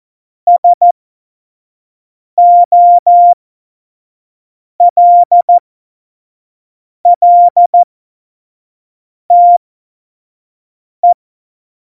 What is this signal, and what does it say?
Morse code, heard as a keyed tone, about 14 wpm, spaced as for 5 wpm, 700 Hz. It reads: SOLLTE